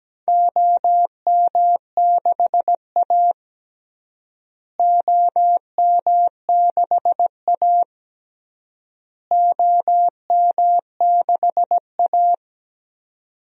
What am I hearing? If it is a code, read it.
Morse code, heard as a keyed tone, about 17 wpm, 700 Hz. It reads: OM6A OM6A OM6A
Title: DXCC-100 Call Signs - 4 Characters - Repeated 3x - 17wpm